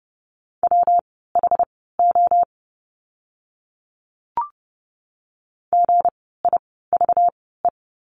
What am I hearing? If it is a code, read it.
Morse code, heard as a keyed tone, about 30 wpm, spaced as for 17 wpm, 700 Hz. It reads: WHO GIVE
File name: course / 30:10wpm / 237 Lesson 38f - Letter q - Words 30wpm